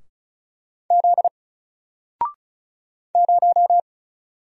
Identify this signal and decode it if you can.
Morse code, heard as a keyed tone, about 34 wpm, 700 Hz. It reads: Z 0
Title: Single Character (Letter:Number:Punctuation:Prosign) 35wpm